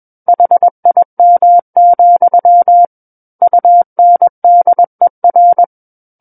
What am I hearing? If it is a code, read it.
Morse code, heard as a keyed tone, about 21 wpm, 700 Hz. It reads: HIM, UNDER
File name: Book One / 12-Chapter 12 - 21 WPM